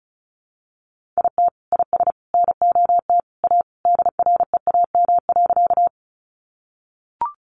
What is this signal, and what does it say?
Morse code, heard as a keyed tone, about 34 wpm, 700 Hz. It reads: IT IS NOT A DREAM.
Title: Sentences from Top 1500 Words 35wpm